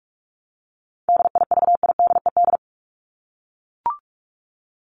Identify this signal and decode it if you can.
Morse code, heard as a keyed tone, about 44 wpm, 700 Hz. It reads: DIVIDED
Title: ED Suffix - Top 100 Words with Base Words 45wpm